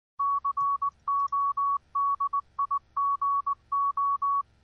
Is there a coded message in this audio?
Yes: Morse code from 0.2 to 4.4 s, heard as a keyed tone, about 19 wpm, 1120 Hz. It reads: CODIGO